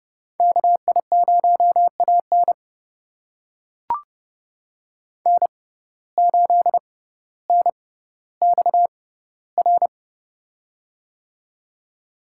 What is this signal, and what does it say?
Morse code, heard as a keyed tone, about 30 wpm, spaced as for 15 wpm, 700 Hz. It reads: KI0AN N8NXR
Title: USA Call Signs - 5 Characters - 6x Character Spacing - 30wpm